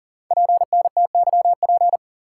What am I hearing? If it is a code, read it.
Morse code, heard as a keyed tone, about 40 wpm, 690 Hz. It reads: PNTYP
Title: Code Groups - 5 Letter:Numbers - 40wpm